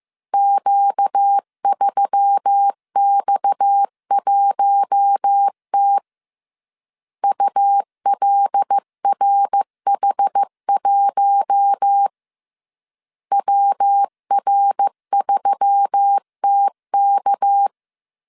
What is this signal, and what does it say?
Morse code, heard as a keyed tone, about 15 wpm, 800 Hz. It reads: Q3X1T ULRH1 WR3TK
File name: CW / DET / 1 6 1500C S C 14wpm 2951